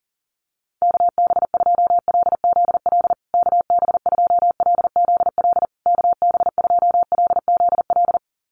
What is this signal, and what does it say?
Morse code, heard as a keyed tone, about 37 wpm, 700 Hz. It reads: KB2LZL KB2LZL KB2LZL
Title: USA Call Signs Repeated 3x 40wpm